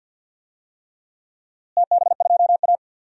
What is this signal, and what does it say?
Morse code, heard as a keyed tone, about 49 wpm, 700 Hz. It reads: TB2A
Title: DXCC-100 Call Signs - 4 Characters - Repeated 3x - 50wpm